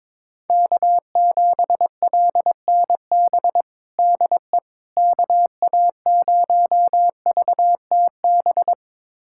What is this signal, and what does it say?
Morse code, heard as a keyed tone, about 22 wpm, 700 Hz. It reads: K7LNB DE KA0VTB
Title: USA Callsign DE Callsign 22wpm